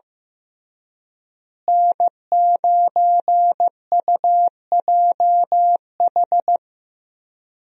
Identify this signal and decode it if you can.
Morse code, heard as a keyed tone, about 16 wpm, 700 Hz. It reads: N9UJH
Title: USA Call Signs - 5 Characters - Repeated 3x - 15wpm